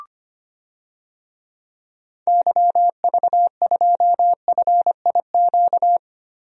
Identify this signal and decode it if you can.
Morse code, heard as a keyed tone, about 25 wpm, 700 Hz. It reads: YV2FIQ
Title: DXCC-100 Call Signs - 6 Characters - 25wpm